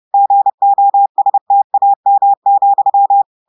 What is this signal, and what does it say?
Morse code, heard as a keyed tone, about 30 wpm, 800 Hz. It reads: GOSTAM,
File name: audios / portugais / portugais texte en clair 2 vitesse 30 mots minutes